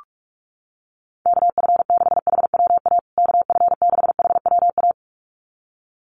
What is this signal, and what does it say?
Morse code, heard as a keyed tone, about 44 wpm, 700 Hz. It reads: KF6HWA KF6HWA
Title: USA Call Signs Repeated 2x 45wpm